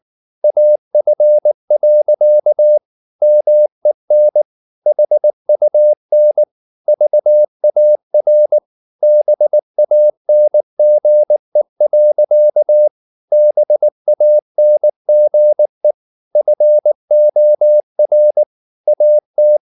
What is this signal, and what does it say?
Morse code, heard as a keyed tone, about 19 wpm, 600 Hz. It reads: AF. MEN HUN VAR BANGE. BANGE FOR AT
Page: Ferie 19 wpm